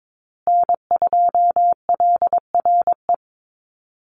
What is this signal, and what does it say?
Morse code, heard as a keyed tone, about 22 wpm, 700 Hz. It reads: N2LRE
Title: USA Call Signs Repeated 3x 22wpm